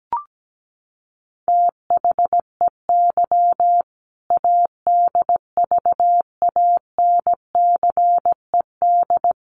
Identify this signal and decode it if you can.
Morse code, heard as a keyed tone, about 17 wpm, 700 Hz. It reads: THEY ADVANCED
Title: Sentences from Top 1200 Words 17wpm